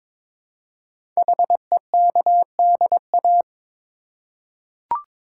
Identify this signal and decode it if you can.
Morse code, heard as a keyed tone, about 22 wpm, 700 Hz. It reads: HEKDA
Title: Code Groups - 5 Letters - 22wpm